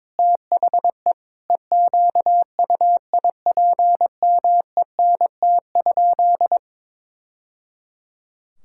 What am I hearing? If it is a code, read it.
Morse code, heard as a keyed tone, about 22 wpm, 700 Hz. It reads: THE EQUIPMENT?
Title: Sentences from Top 1000 Words - Review Part 5 - 22wpm